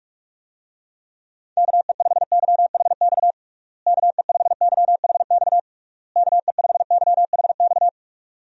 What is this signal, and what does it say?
Morse code, heard as a keyed tone, about 41 wpm, 700 Hz. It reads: KE5YHX KE5YHX KE5YHX
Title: USA Call Signs - 6 Characters - Repeated 3x - 45wpm